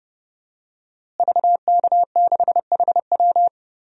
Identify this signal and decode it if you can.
Morse code, heard as a keyed tone, about 31 wpm, 700 Hz. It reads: VK6HW
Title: DXCC-100 Call Signs - 5 Characters - Repeated 2x - 30wpm